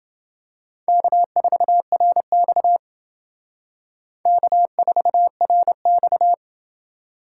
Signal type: Morse code

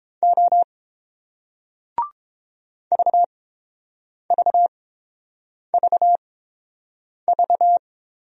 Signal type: Morse code

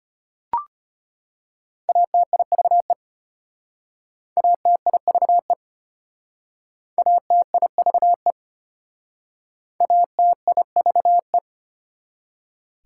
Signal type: Morse code

